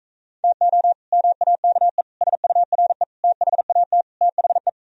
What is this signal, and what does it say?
Morse code, heard as a keyed tone, about 45 wpm, 700 Hz. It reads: TO MAKE SURE THAT THE